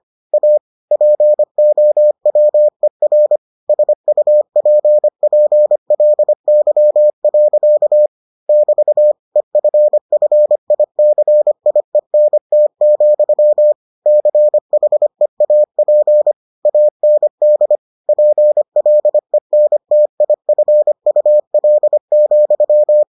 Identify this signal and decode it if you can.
Morse code, heard as a keyed tone, about 25 wpm, 600 Hz. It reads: A POWER SUPPLY. <BT> EFFICIENT, CHEAP AND PLENTIFUL,